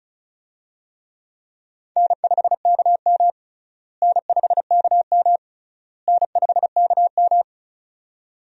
Morse code, heard as a keyed tone, about 34 wpm, 700 Hz. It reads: N5KM N5KM N5KM
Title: USA Call Signs - 4 Characters - Repeated 3x - 35wpm